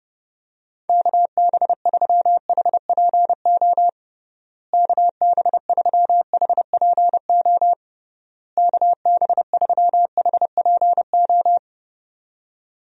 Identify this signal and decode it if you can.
Morse code, heard as a keyed tone, about 30 wpm, 700 Hz. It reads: KB3HPO KB3HPO KB3HPO
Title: USA Call Signs - 6 Characters - Repeated 3x - 30wpm